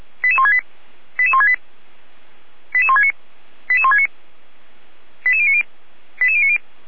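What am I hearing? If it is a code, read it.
Five-tone selective call (ZVEI), heard as quick runs of five tones, tones of about 70 ms. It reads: 88168 88168 88169 88169 88090 88090
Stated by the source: ZVEI five-tone call 88521, received by SDR.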